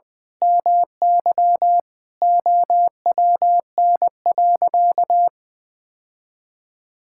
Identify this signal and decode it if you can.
Morse code, heard as a keyed tone, about 20 wpm, 700 Hz. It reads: MY OWN.
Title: Sentences from Top 1000 Words - Review Part 6 - 20wpm